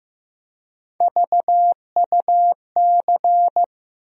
Morse code, heard as a keyed tone, about 15 wpm, 700 Hz. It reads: VUC
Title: USA Call Sign Postfixes - 3 Letters 15wpm